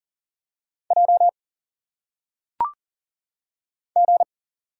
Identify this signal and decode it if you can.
Morse code, heard as a keyed tone, about 40 wpm, 700 Hz. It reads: J G
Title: Single Letters - Phonetic Pronunciation 40wpm